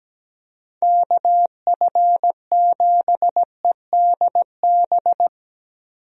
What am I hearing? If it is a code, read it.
Morse code, heard as a keyed tone, about 17 wpm, 700 Hz. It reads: KF7EDB
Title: USA Call Signs - 6 Characters - Repeated 2x - 17wpm